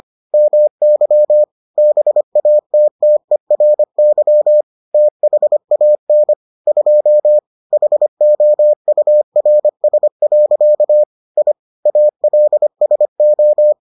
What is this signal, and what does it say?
Morse code, heard as a keyed tone, about 25 wpm, 600 Hz. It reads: MY BATTERY THAN 2 HOURS. I ALSO